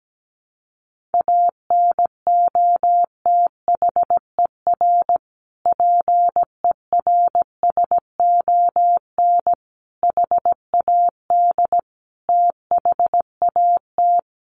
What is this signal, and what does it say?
Morse code, heard as a keyed tone, about 17 wpm, 700 Hz. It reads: ANOTHER PERSON HAD THAT
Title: Sentences from Top 1400 Words 17wpm